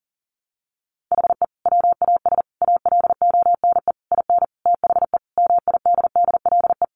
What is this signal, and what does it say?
Morse code, heard as a keyed tone, about 40 wpm, 700 Hz. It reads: HE WAS ALONE IN THE MIDDLE
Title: Sentences from Top 1000 Words 40wpm